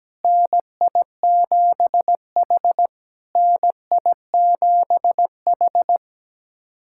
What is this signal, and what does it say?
Morse code, heard as a keyed tone, about 17 wpm, 700 Hz. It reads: NI7H NI7H
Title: SOTA QSO Practice 17wpm